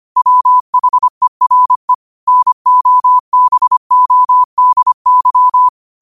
Morse code, heard as a keyed tone, about 25 wpm, 1000 Hz. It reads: WHERE NOBODY